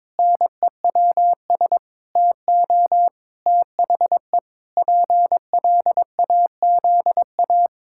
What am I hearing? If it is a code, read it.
Morse code, heard as a keyed tone, about 22 wpm, 700 Hz. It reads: NEWS TO THE PLAZA
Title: Pangram Challenge v2 - 22wpm